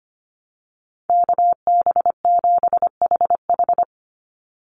Morse code, heard as a keyed tone, about 25 wpm, 700 Hz. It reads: KB7HH
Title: NA SOTA Chaser Callsigns 25wpm